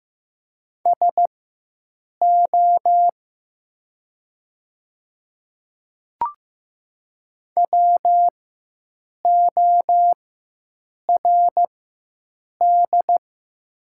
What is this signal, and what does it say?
Morse code, heard as a keyed tone, about 15 wpm, spaced as for 7 wpm, 700 Hz. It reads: SO WORD
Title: 4x Character Spacing - Top 100 Words - 15wpm